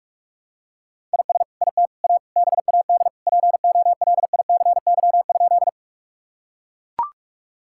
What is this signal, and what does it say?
Morse code, heard as a keyed tone, about 41 wpm, 700 Hz. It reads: IS IT A BAD POLICY?
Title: Sentences from Top 1000 Words - Review Part 5 - 45wpm